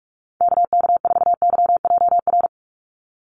Morse code, heard as a keyed tone, about 44 wpm, 700 Hz. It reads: KK4YJR
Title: USA Call Signs 45wpm